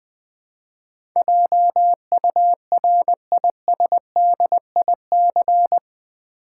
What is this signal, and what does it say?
Morse code, heard as a keyed tone, about 20 wpm, 700 Hz. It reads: JURISDIC
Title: Build A Word - 4 Letters At A Time - 12-Letter Words - 20wpm